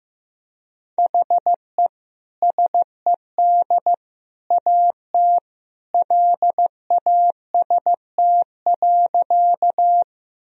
Morse code, heard as a keyed tone, about 15 wpm, 700 Hz. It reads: HE SED AT LAST.
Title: Sentences from Top 300 Words and Abbreviations 15wpm